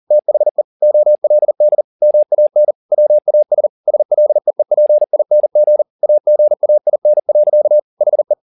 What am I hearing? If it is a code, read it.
Morse code, heard as a keyed tone, about 40 wpm, 600 Hz. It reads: THE OLD MAN WAS SLEEPING AGAIN. HE
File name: the old man and the sea 2 40wpm